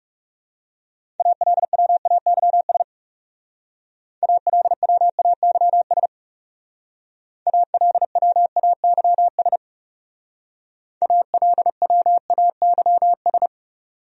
Morse code, heard as a keyed tone, about 39 wpm, 700 Hz. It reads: ALWAYS ALWAYS ALWAYS ALWAYS